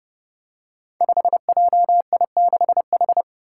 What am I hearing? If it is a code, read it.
Morse code, heard as a keyed tone, about 30 wpm, 700 Hz. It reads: 5JI6H
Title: Five Challenging Letter-Number Combinations 30wpm